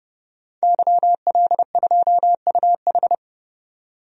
Morse code, heard as a keyed tone, about 30 wpm, 700 Hz. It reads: YL2UH
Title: DXCC-100 Call Signs - 5 Characters - 30wpm